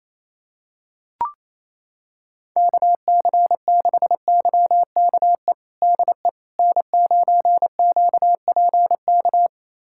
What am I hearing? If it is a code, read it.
Morse code, heard as a keyed tone, about 28 wpm, 700 Hz. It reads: KC6YKE DE N9QPK
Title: USA Callsign DE Callsign 28wpm